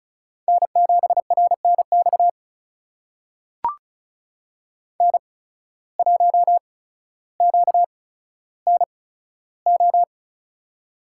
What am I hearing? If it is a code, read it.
Morse code, heard as a keyed tone, about 36 wpm, spaced as for 15 wpm, 700 Hz. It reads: N7RNX N1QNO
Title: USA Call Signs - 5 Characters - 8x Character Spacing - 35wpm